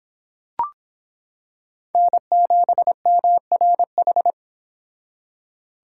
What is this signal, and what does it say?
Morse code, heard as a keyed tone, about 26 wpm, 700 Hz. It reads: N7MRH